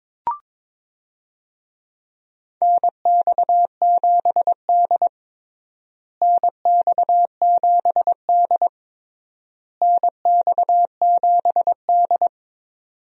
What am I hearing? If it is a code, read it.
Morse code, heard as a keyed tone, about 22 wpm, 700 Hz. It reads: NX7D NX7D NX7D
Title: USA Call Signs - 4 Characters - Repeated 3x - 22wpm